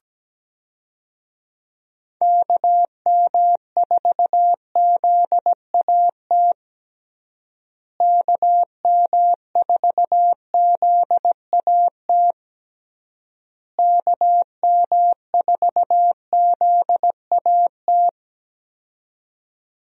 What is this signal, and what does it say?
Morse code, heard as a keyed tone, about 17 wpm, 700 Hz. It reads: KM4ZAT KM4ZAT KM4ZAT
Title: USA Call Signs - 6 Characters - Repeated 3x - 17wpm